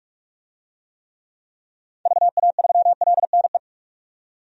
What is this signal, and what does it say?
Morse code, heard as a keyed tone, about 46 wpm, 700 Hz. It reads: VA3LNE
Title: DXCC-100 Call Signs - 6 Characters - 45wpm